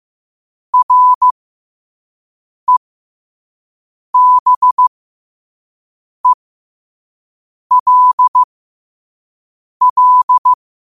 Morse code, heard as a keyed tone, about 15 wpm, spaced as for 5 wpm, 1000 Hz. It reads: REBELL
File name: qotdfiles / 5wpm